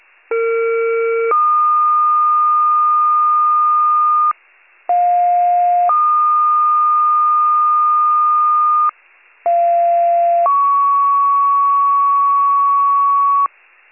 Two-tone page, heard as a one-second tone, then a three-second tone, about 460 then 1220 Hz, about 710 then 1220 Hz, about 690 then 1120 Hz. It is heard from 0.3 to 13.5 s.